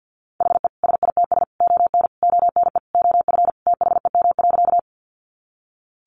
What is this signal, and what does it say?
Morse code, heard as a keyed tone, about 50 wpm, 700 Hz. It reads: HE SITS ON ONE OF THEM.
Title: Additional Easy Phrases 50wpm